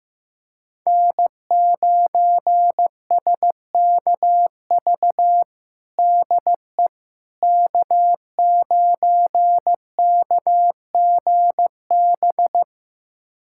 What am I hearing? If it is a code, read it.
Morse code, heard as a keyed tone, about 15 wpm, 700 Hz. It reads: N9SKV DE K9KGB
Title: USA Callsign DE Callsign - 5 Characters - 15wpm